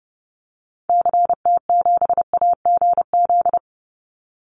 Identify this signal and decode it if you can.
Morse code, heard as a keyed tone, about 30 wpm, 700 Hz. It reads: CT7AGZ
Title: Call Signs Repeated 2x - 30wpm